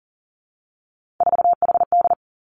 Morse code, heard as a keyed tone, about 39 wpm, 700 Hz. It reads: 4HD
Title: Three Challenging Letter-Number Combinations 40wpm